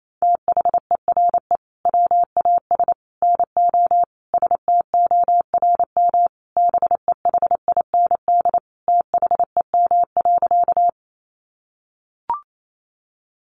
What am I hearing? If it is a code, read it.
Morse code, heard as a keyed tone, about 28 wpm, 700 Hz. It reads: THERE WAS NO STORM BEHIND THEM.
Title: Sentences from Top 1500 Words 28wpm